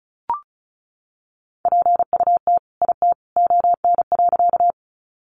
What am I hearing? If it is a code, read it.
Morse code, heard as a keyed tone, about 34 wpm, 700 Hz. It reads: PUT IT ON.
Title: Sentences from Top 200 Words 35wpm